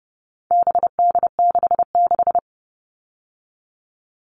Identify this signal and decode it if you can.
Morse code, heard as a keyed tone, about 30 wpm, 700 Hz. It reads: BD66
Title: All Cut Numbers - 4-Digit 30wpm